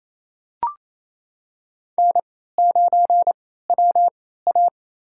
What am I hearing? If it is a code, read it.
Morse code, heard as a keyed tone, about 27 wpm, spaced as for 16 wpm, 700 Hz. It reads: N9WA
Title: USA Call Signs - 4 Characters - 3x Character Spacing - 28wpm